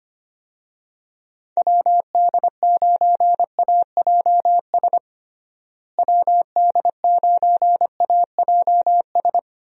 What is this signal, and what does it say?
Morse code, heard as a keyed tone, about 25 wpm, 700 Hz. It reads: WD9AJS WD9AJS
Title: USA Call Signs - 6 Characters - Repeated 3x - 25wpm